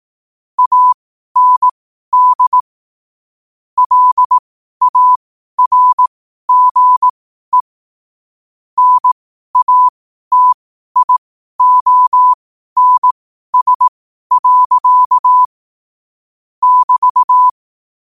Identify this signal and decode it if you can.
Morse code, heard as a keyed tone, about 18 wpm, spaced as for 12 wpm, 1000 Hz. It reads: AND LARGE NATIONS. <BT>
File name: qotdfiles / 12wpm